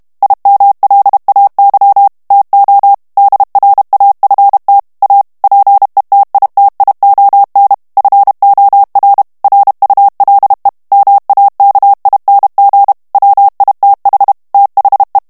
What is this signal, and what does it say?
Morse code, heard as a keyed tone, about 32 wpm, 800 Hz. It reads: IMLAY TO DRAFT A PETITION FOR RULE MAKING WITH THE
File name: morse mp3 / ANGLAIS texte clair 3 vitesse 30 mots minute